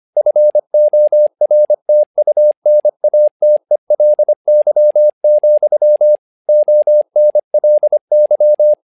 Morse code, heard as a keyed tone, about 25 wpm, 600 Hz. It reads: FORTUNATELY, ONLY
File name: learning center / tower trouble / tower trouble 25wpm